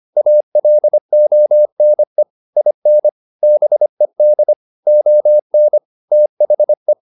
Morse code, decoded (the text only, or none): ALONE IN BED ON THE